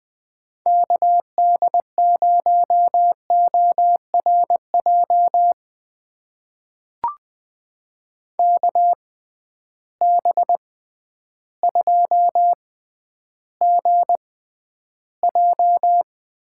Morse code, heard as a keyed tone, about 21 wpm, spaced as for 11 wpm, 700 Hz. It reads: KD0ORJ KB2GJ
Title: USA Call Signs - 6 Characters - 6x Character Spacing - 20wpm